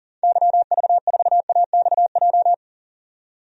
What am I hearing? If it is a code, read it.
Morse code, heard as a keyed tone, about 40 wpm, 700 Hz. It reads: YV4AXJ
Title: DXCC-100 Call Signs - 6 Characters - 40wpm